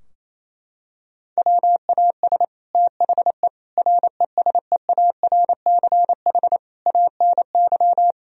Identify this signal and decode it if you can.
Morse code, heard as a keyed tone, about 28 wpm, 700 Hz. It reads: WAS THE RESEARCH ANY
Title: Sentences from Top 1000 Words - Review Part 4 - 28wpm